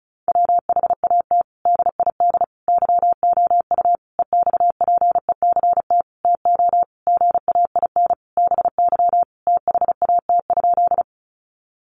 Morse code, heard as a keyed tone, about 34 wpm, 700 Hz. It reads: WHAT DID YOU EXPECT TO GAIN BY THAT?
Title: Sentences from Top 1100 Words 35wpm